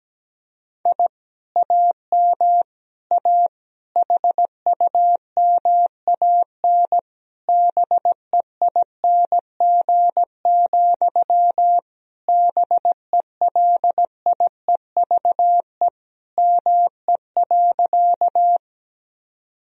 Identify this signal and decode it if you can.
Morse code, heard as a keyed tone, about 17 wpm, 700 Hz. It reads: I AM A HUMAN BEING, BELIEVE ME.
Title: Sentences from Top 1000 Words - Review Part 3 - 17wpm